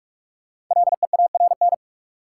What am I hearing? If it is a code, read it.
Morse code, heard as a keyed tone, about 46 wpm, 700 Hz. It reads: LEARN
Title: Sentences from Top 300 Words - New Word Drill - 45wpm